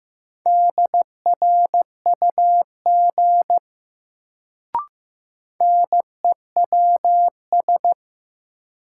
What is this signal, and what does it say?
Morse code, heard as a keyed tone, about 15 wpm, 700 Hz. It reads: DRUG NEWS
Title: Sets of 5 Words - New Word Drill - 15wpm